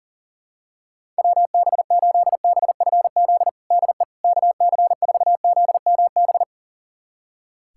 Morse code, heard as a keyed tone, about 40 wpm, 700 Hz. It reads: WB8BFZ DE KC4ZMB
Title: USA Callsign DE Callsign - 6 Characters - 40wpm